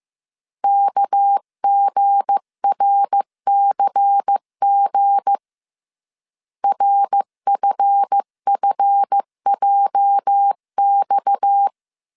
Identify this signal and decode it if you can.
Morse code, heard as a keyed tone, about 15 wpm, 800 Hz. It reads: KGRCG RFFJX
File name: CW / DET / 1 9 1500C S B 14wpm 3302